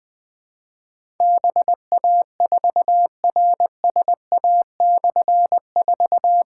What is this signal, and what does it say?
Morse code, heard as a keyed tone, about 20 wpm, 700 Hz. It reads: BA4RSA/4
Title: Difficult Call Signs 20wpm